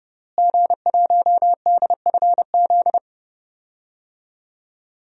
Morse code, heard as a keyed tone, about 30 wpm, 700 Hz. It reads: G1DFZ
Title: DXCC-100 Call Signs - 5 Characters - Repeated 3x - 30wpm